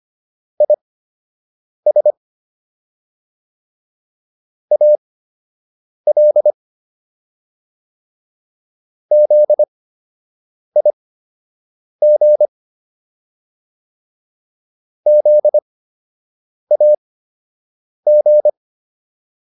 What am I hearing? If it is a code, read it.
Morse code, heard as a keyed tone, about 25 wpm, spaced as for 7 wpm, 600 Hz. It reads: IS AL ZIG ZAG